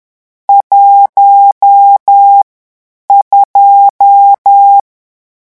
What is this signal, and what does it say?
Morse code, heard as a keyed tone, about 10 wpm, 800 Hz. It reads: 1 2
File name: morse mp3 / lecon10-vitesse 10